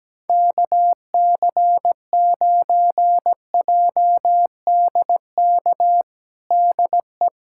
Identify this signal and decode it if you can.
Morse code, heard as a keyed tone, about 17 wpm, 700 Hz. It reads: KC9JDK DE
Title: USA Callsign DE Callsign - 6 Characters - 17wpm